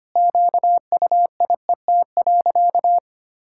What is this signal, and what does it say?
Morse code, heard as a keyed tone, about 25 wpm, 700 Hz. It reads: QUIET.
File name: practice / 25wpm / Sentences with 3-Word Phrases - 25wpm